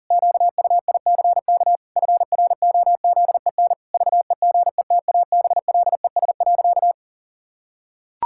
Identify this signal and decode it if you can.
Morse code, heard as a keyed tone, about 40 wpm, 700 Hz. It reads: QUICK FROZEN VEGETABLES.
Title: Pangram Challenge v2 - 40wpm